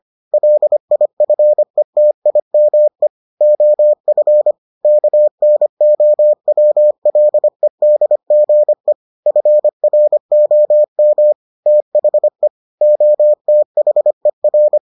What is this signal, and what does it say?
Morse code, heard as a keyed tone, about 25 wpm, 600 Hz. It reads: LIFETIME OF KNOWLEDGE FROM THE OTHER